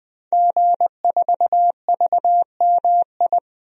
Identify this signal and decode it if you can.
Morse code, heard as a keyed tone, about 20 wpm, 700 Hz. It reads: G4VMI
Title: DXCC-100 Call Signs - 5 Characters - 20wpm